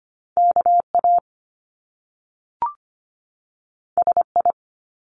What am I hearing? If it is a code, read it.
Morse code, heard as a keyed tone, about 24 wpm, 700 Hz. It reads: KA SI
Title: Top Two Letter Combinations in Words 25wpm